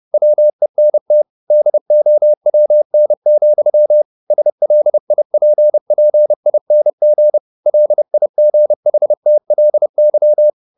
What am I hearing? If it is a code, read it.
Morse code, heard as a keyed tone, about 30 wpm, 600 Hz. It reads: WENT DOWN, SLIPPING LIGHTLY
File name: the old man and the sea / the old man and the sea 1 30wpm